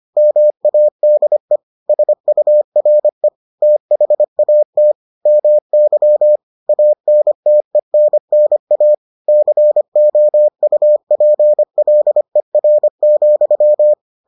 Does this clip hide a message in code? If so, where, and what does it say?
Yes: Morse code from 0.2 to 13.9 s, heard as a keyed tone, about 25 wpm, 600 Hz. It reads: MADE SURE THAT MY ANTENNA COUPLER,